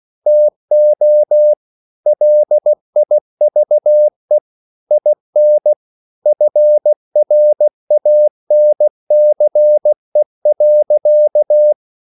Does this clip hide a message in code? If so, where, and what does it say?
Yes: Morse code from 0.3 to 11.7 s, heard as a keyed tone, about 16 wpm, 600 Hz. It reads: TO LIVE IN FRANCE.